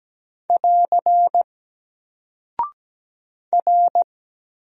Morse code, heard as a keyed tone, about 17 wpm, 700 Hz. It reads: <AR> R
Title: Single Character (Letter:Number:Punctuation:Prosign) Alt 17wpm